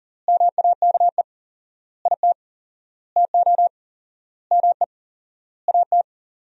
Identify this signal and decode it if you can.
Morse code, heard as a keyed tone, about 40 wpm, 700 Hz. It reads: MAKE IT TO ME AT
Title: 4x Word-Spacing - Sentences from Top 500 Words - 40wpm